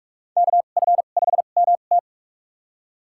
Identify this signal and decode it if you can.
Morse code, heard as a keyed tone, about 46 wpm, spaced as for 32 wpm, 700 Hz. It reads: KF5MT
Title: USA Call Signs - 5 Characters - 2x Character Spacing - 45wpm